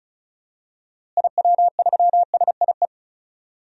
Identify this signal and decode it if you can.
Morse code, heard as a keyed tone, about 36 wpm, 700 Hz. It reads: IW3SIE
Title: DXCC-100 Call Signs - 6 Characters - Repeated 3x - 35wpm